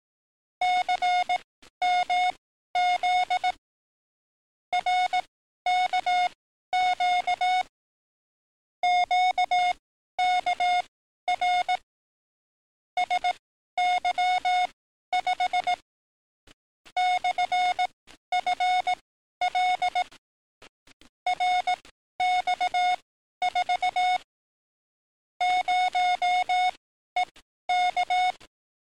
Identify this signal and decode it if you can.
Morse code, heard as a keyed tone, about 18 wpm, spaced as for 12 wpm, 720 Hz. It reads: CMZ RKQ QKR SY5 /FL RX4 0EK
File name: training / NAVPERS 11049to11060 / NAVPERS 11052 RA RB